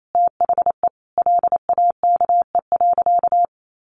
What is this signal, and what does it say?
Morse code, heard as a keyed tone, about 28 wpm, 700 Hz. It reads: THE LAKE.
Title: Sentences from Top 1100 Words 28wpm